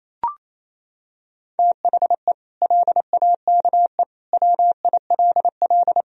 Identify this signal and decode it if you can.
Morse code, heard as a keyed tone, about 28 wpm, 700 Hz. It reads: THE LAKE WILL
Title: Sentences from Top 1000 Words - Review Part 3 - 28wpm